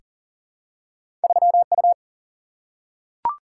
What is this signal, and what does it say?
Morse code, heard as a keyed tone, about 40 wpm, 700 Hz. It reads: 3U
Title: Single-Letter Single-Number 40wpm